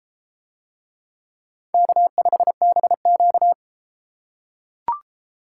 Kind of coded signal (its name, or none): Morse code